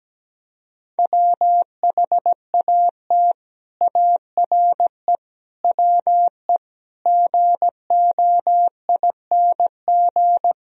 Morse code, heard as a keyed tone, about 17 wpm, 700 Hz. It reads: WHAT ARE WE GOING
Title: Speed-Racing - Sets of 5 Words 17wpm